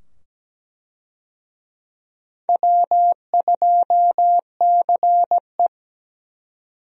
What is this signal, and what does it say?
Morse code, heard as a keyed tone, about 17 wpm, 700 Hz. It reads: W2CE
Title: USA Call Signs - 4 Characters - 17wpm